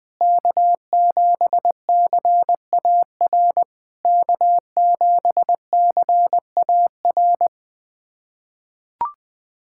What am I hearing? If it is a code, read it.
Morse code, heard as a keyed tone, about 20 wpm, 700 Hz. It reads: K7CAR K7CAR
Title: POTA QSO Practice - 20wpm